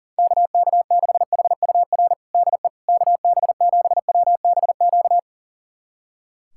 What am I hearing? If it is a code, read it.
Morse code, heard as a keyed tone, about 40 wpm, 700 Hz. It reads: KK6HUR DE KB7WBQ
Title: USA Callsign DE Callsign - 6 Characters - 40wpm